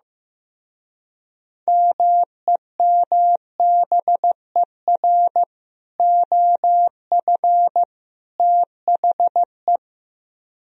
Morse code, heard as a keyed tone, about 15 wpm, 700 Hz. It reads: MEMBER OF THE
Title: Speed-Racing - Sets of 3 Words 15wpm